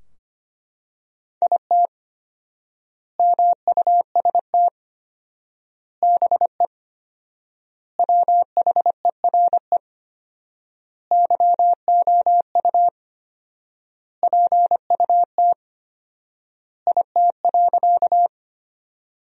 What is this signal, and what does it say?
Morse code, heard as a keyed tone, about 25 wpm, 700 Hz. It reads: IT MUST BE WHERE YOU PUT IT.
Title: 4x Word-Spacing - Sentences from Top 200 Words - 25wpm